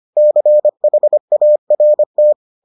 Morse code, heard as a keyed tone, about 25 wpm, 600 Hz. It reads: CHART